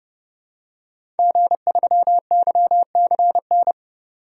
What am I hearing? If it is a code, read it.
Morse code, heard as a keyed tone, about 31 wpm, 700 Hz. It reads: G3YCN
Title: DXCC-100 Call Signs - 5 Characters - 30wpm